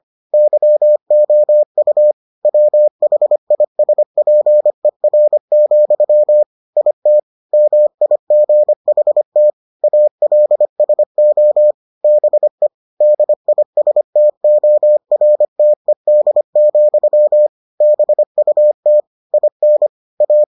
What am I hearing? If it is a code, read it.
Morse code, heard as a keyed tone, about 25 wpm, 600 Hz. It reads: YOU WHISPER, IT MIGHT ALSO BE DISTORTED, BUT IN A